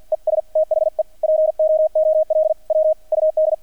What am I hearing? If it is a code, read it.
Morse code, heard as a keyed tone, about 50 wpm, spaced as for 43 wpm, 630 Hz. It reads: ES THE POOL WUD
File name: archived / CRSnet